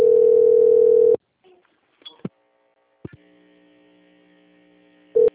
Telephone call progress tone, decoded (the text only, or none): ringback tone